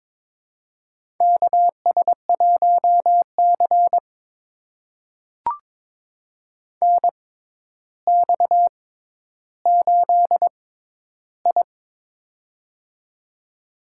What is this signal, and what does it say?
Morse code, heard as a keyed tone, about 22 wpm, spaced as for 11 wpm, 700 Hz. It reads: KS1C NX8I
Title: USA Call Signs - 4 Characters - 6x Character Spacing - 22wpm